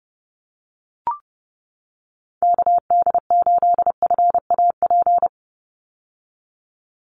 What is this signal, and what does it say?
Morse code, heard as a keyed tone, about 29 wpm, 700 Hz. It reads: KD8FAP
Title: USA Call Signs 30wpm